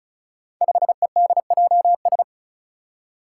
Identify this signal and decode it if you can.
Morse code, heard as a keyed tone, about 36 wpm, 700 Hz. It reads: 5EDJS
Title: Five Challenging Letter-Number Combinations 35wpm